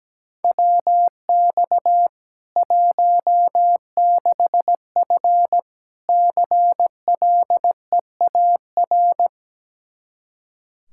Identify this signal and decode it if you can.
Morse code, heard as a keyed tone, about 17 wpm, 700 Hz. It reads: WX 16F CLEAR
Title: K1USN SST QSO Practice 17wpm